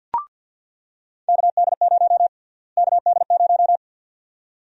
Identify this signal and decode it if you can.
Morse code, heard as a keyed tone, about 49 wpm, 700 Hz. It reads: KD0 KD0
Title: USA Call Sign Prefixes - Repeated 2x - 50wpm